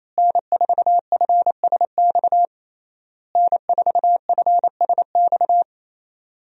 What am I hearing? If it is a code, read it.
Morse code, heard as a keyed tone, about 28 wpm, 700 Hz. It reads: N4FSX N4FSX
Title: USA Call Signs - 5 Characters - Repeated 3x - 28wpm